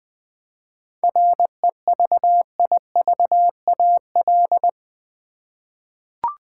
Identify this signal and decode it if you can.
Morse code, heard as a keyed tone, about 20 wpm, 700 Hz. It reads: REVIVAL
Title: 7 Letter Words - First 3 Letters Given - 20wpm